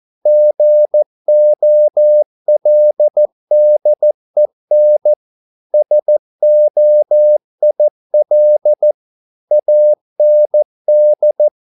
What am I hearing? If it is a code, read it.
Morse code, heard as a keyed tone, about 14 wpm, 600 Hz. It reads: GOLDEN SOIL AND